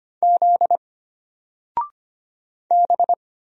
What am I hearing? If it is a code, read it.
Morse code, heard as a keyed tone, about 25 wpm, 700 Hz. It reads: Z B
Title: Single Letter-Number 25wpm